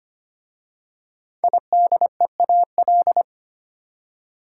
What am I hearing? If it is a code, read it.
Morse code, heard as a keyed tone, about 25 wpm, 700 Hz. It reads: IDEAL
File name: 5 Letter Words - First 3 Letters Given - 25wpm